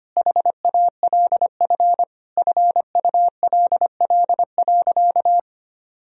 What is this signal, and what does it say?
Morse code, heard as a keyed tone, about 25 wpm, 700 Hz. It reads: HALF FULL.
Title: Sentences from Top 1000 Words - Review Part 3 - 25wpm